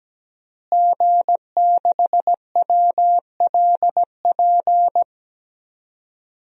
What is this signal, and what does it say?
Morse code, heard as a keyed tone, about 17 wpm, 700 Hz. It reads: G6WLP
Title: DXCC-100 Call Signs - 5 Characters - Repeated 2x - 17wpm